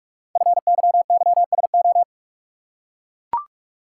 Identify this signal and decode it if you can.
Morse code, heard as a keyed tone, about 44 wpm, 700 Hz. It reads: FYYSO
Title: Code Groups - 5 Letters - 45wpm